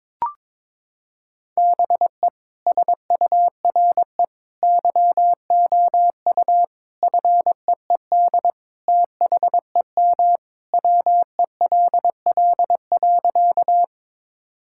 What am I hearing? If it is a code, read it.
Morse code, heard as a keyed tone, about 22 wpm, 700 Hz. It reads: BE SURE YOU FEED THEM WELL.
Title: Sentences from Top 1600 Words 22wpm